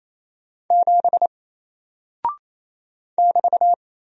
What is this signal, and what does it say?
Morse code, heard as a keyed tone, about 27 wpm, 700 Hz. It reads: 7 <BT>
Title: Single Character (Letter:Number:Punctuation:Prosign) 28wpm